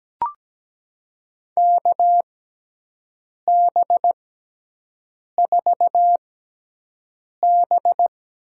Morse code, heard as a keyed tone, about 17 wpm, spaced as for 6 wpm, 700 Hz. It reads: KB4B
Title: USA Call Signs - 6 Characters - 6x Character Spacing - 17wpm